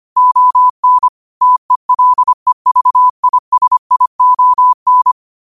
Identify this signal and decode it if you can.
Morse code, heard as a keyed tone, about 25 wpm, 1000 Hz. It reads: ON TELEVISION
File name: qotdfiles / 25wpm